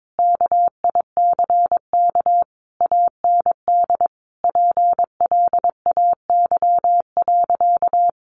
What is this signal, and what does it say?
Morse code, heard as a keyed tone, about 22 wpm, 700 Hz. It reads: KICK AND PLAY.
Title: Sentences from Top 1300 Words 22wpm